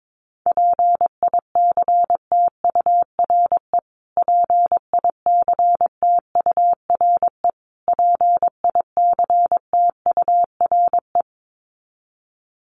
Morse code, heard as a keyed tone, about 22 wpm, 700 Hz. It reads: PICTURE PICTURE PICTURE
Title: Top 300 Words Repeated 22wpm